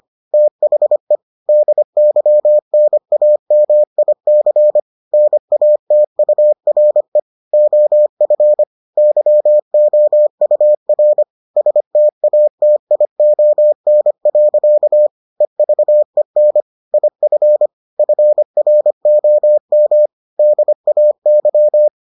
Morse code, heard as a keyed tone, about 25 wpm, 600 Hz. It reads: THE DYNAMIC NATURE OF YOUR STATION. EVEN IF FROM DAY